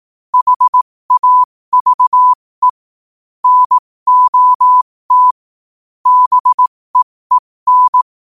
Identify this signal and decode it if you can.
Morse code, heard as a keyed tone, about 18 wpm, spaced as for 15 wpm, 1000 Hz. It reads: HAVE NOT BEEN
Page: New quotes every day in morse code at 15 Words per minute.